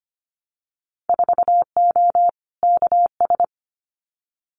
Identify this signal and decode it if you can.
Morse code, heard as a keyed tone, about 25 wpm, 700 Hz. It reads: 4O KS
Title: Winter Field Day Exchanges Encore 25wpm